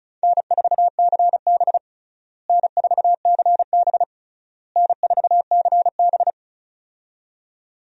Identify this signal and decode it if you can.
Morse code, heard as a keyed tone, about 34 wpm, 700 Hz. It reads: N4CB N4CB N4CB
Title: USA Call Signs - 4 Characters - Repeated 3x - 35wpm